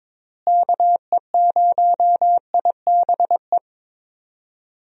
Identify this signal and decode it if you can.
Morse code, heard as a keyed tone, about 22 wpm, 700 Hz. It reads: KE0IBE
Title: USA Call Signs - 6 Characters - Repeated 3x - 22wpm